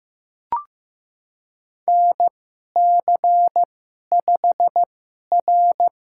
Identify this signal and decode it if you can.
Morse code, heard as a keyed tone, about 15 wpm, spaced as for 11 wpm, 690 Hz. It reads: NC5R
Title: USA Call Signs - 4 Characters - 2x Character Spacing - 15wpm